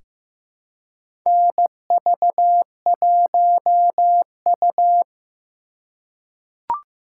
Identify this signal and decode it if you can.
Morse code, heard as a keyed tone, about 15 wpm, 700 Hz. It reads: NV1U
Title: K1USN SST QSO Practice - 15wpm